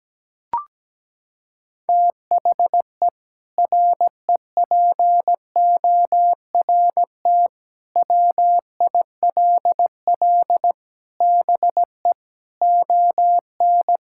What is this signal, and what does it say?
Morse code, heard as a keyed tone, about 17 wpm, 700 Hz. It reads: THE REPORT WILL BE ON
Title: Sentences from Top 1000 Words - Review Part 6 - 17wpm